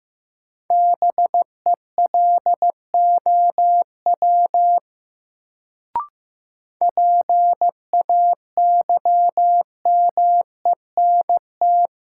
Morse code, heard as a keyed tone, about 15 wpm, 700 Hz. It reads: BELOW PAYMENT
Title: Sentences from Top 1200 Words - New Word Drill Alt - 15wpm